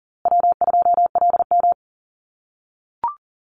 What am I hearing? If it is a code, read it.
Morse code, heard as a keyed tone, about 40 wpm, 700 Hz. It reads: W2LM
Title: USA Call Signs Repeated 3x 40wpm